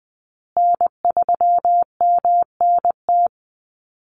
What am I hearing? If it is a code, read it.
Morse code, heard as a keyed tone, about 20 wpm, 700 Hz. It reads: N3MNT
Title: Call Signs Repeated 2x - 20wpm